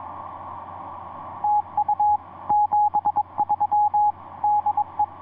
Morse code, heard as a keyed tone, about 22 wpm, 840 Hz, with radio noise. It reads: TU 73 DE
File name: ce1-k7ca ea8tl etc 17m cw 10jan2010 1620